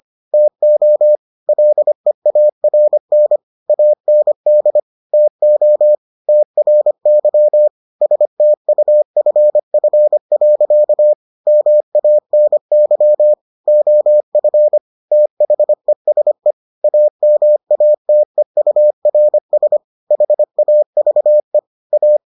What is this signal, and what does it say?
Morse code, heard as a keyed tone, about 25 wpm, 600 Hz. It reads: TO LEARN AND TO TRY STUFF. MANY OF THESE AMATEURS HAVE A